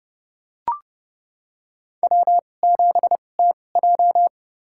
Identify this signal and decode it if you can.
Morse code, heard as a keyed tone, about 30 wpm, spaced as for 22 wpm, 700 Hz. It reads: W7TJ
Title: USA Call Signs - 2x Character Spacing - 30wpm